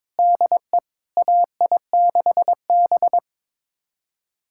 Morse code, heard as a keyed tone, about 22 wpm, 700 Hz. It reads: DE AI6B
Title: USA Callsign DE Callsign - 4 Characters - 22wpm